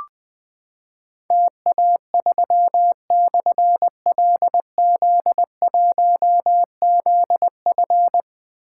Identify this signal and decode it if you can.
Morse code, heard as a keyed tone, about 20 wpm, 700 Hz. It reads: TA3/LZ1ZF
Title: Difficult Call Signs 20wpm